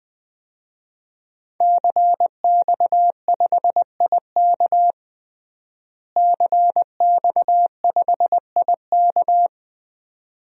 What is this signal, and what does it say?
Morse code, heard as a keyed tone, about 20 wpm, 700 Hz. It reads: CX5IK CX5IK
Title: DXCC-100 Call Signs - 5 Characters - Repeated 2x - 20wpm